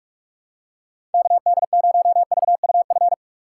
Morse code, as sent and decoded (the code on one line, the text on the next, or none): -.- -.. ----- ...- ..- ..-.
KD0VUF